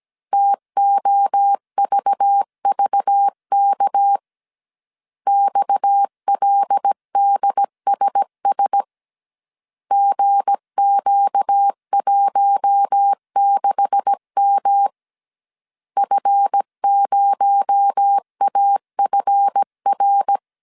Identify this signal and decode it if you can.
Morse code, heard as a keyed tone, about 17 wpm, 800 Hz. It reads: TOVVK XLDSS GQ16M F0AFR
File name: CW / DET / 1 9 1500C S C 16wpm 2909